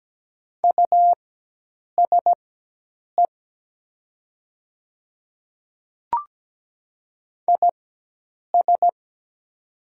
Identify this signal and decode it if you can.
Morse code, heard as a keyed tone, about 17 wpm, spaced as for 8 wpm, 700 Hz. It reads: USE IS
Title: 4x Character Spacing - Top 100 Words - 17wpm